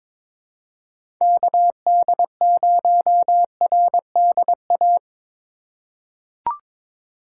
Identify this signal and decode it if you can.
Morse code, heard as a keyed tone, about 22 wpm, 700 Hz. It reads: KD0RDA
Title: USA Call Signs - 4x Character Spacing - 22wpm